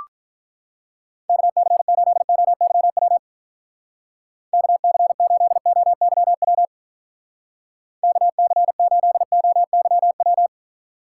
Morse code, heard as a keyed tone, about 45 wpm, 700 Hz. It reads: KC8OYW KC8OYW KC8OYW